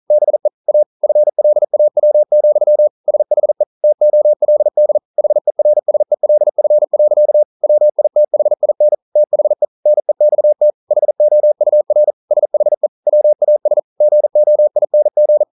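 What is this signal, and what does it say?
Morse code, heard as a keyed tone, about 42 wpm, 600 Hz. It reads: BE A FLAW, SHE TOLD HERSELF. WITHIN THE NEXT HOUR SHE WAS GOING
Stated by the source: Morse code practice at 41 wpm, from CW med Gnister.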